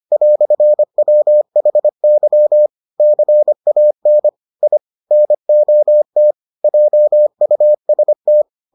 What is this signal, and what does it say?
Morse code, heard as a keyed tone, about 25 wpm, 600 Hz. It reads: "WHY CAN I NOT JUST